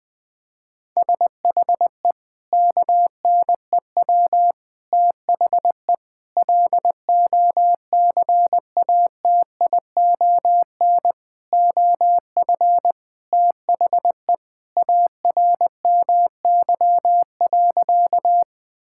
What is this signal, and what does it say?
Morse code, heard as a keyed tone, about 20 wpm, 700 Hz. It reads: SHE KNEW THE LOCATION OF THE ARMY.
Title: Sentences from Top 1000 Words - Review Part 6 - 20wpm